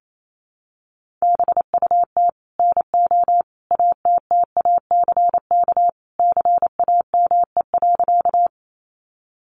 Morse code, heard as a keyed tone, about 28 wpm, 700 Hz. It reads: BUT NO ATTACK CAME.
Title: Sentences from Top 1500 Words 28wpm